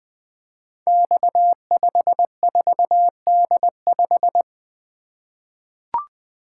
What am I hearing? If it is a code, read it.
Morse code, heard as a keyed tone, about 20 wpm, 700 Hz. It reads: X54D5
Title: Code Groups - 5 Letter:Numbers - 20wpm